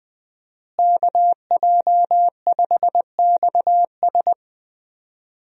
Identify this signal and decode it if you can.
Morse code, heard as a keyed tone, about 20 wpm, 700 Hz. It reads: KJ5XS
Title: USA Call Signs - 5 Characters - Repeated 2x - 20wpm